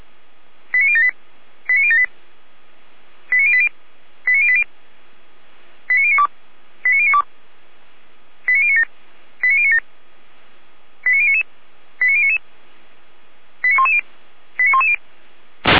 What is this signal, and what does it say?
Five-tone selective call (ZVEI), heard as quick runs of five tones, tones of about 70 ms. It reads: 89078 89078 89080 89080 89092 89092 89097 89097 89099 89099 89110 89110